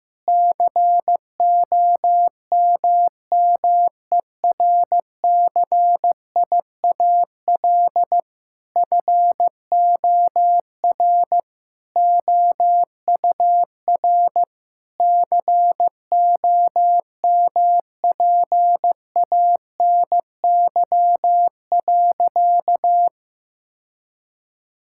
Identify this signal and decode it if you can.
Morse code, heard as a keyed tone, about 15 wpm, 700 Hz. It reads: COMMERCIAL FOR OUR COMPANY.
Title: Sentences from Top 1000 Words - Review Part 5 - 15wpm